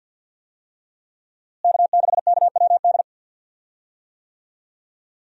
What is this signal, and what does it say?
Morse code, heard as a keyed tone, about 47 wpm, 700 Hz. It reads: K6KWD
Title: USA Call Signs - 5 Characters - Repeated 3x - 50wpm